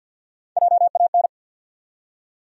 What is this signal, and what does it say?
Morse code, heard as a keyed tone, about 52 wpm, 700 Hz. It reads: JAN
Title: USA Common Names - 50wpm